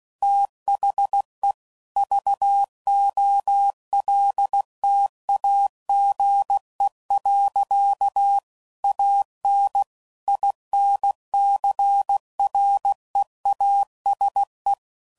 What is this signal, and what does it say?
Morse code, heard as a keyed tone, about 16 wpm, 800 Hz. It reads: THE VOLTAGE. AN INCREASE
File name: CW Code Groups @15WPM / Technical Subjects / XC